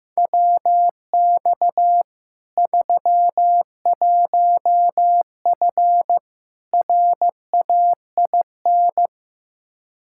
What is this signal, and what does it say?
Morse code, heard as a keyed tone, about 15 wpm, 700 Hz. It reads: WX 31F RAIN
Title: Practice QSO - Exchange 2 - Info - 15wpm